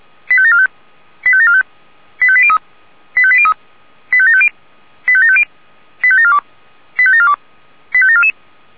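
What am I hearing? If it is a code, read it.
Five-tone selective call (ZVEI), heard as quick runs of five tones, tones of about 70 ms. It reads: 85745 85745 85803 85803 85750 85750 85742 85742 85744
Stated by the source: Five-tone ZVEI selective call, received by SDR.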